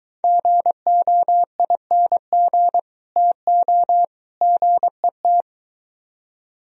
Morse code, heard as a keyed tone, about 23 wpm, 700 Hz. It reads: GOING TO GET